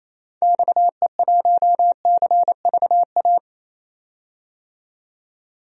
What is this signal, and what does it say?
Morse code, heard as a keyed tone, about 28 wpm, 700 Hz. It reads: XE1CVA
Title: DXCC-100 Call Signs - 6 Characters - 28wpm